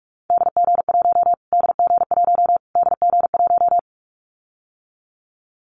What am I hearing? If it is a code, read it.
Morse code, heard as a keyed tone, about 44 wpm, 700 Hz. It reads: DG1 DG1 DG1
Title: Callsign Prefixes for Top 25 DXCC - Repeated 3x - 45wpm